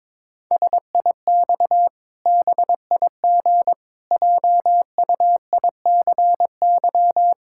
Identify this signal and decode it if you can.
Morse code, heard as a keyed tone, about 22 wpm, 700 Hz. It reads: SIX BIG JUICY
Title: Pangram Challenge v2 - 22wpm